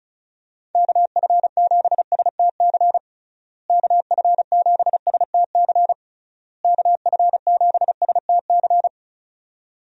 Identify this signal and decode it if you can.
Morse code, heard as a keyed tone, about 36 wpm, 700 Hz. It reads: KF7STC KF7STC KF7STC
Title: USA Call Signs - 6 Characters - Repeated 3x - 35wpm